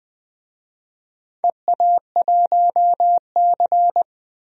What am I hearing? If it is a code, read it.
Morse code, heard as a keyed tone, about 20 wpm, 700 Hz. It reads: EA1C
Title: DXCC-100 Call Signs - 4 Characters - 20wpm